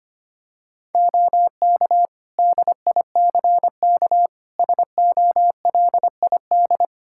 Morse code, heard as a keyed tone, about 25 wpm, 700 Hz. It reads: OK DICK SOLID
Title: Practice QSO - Exchange 3 - More Info - 25wpm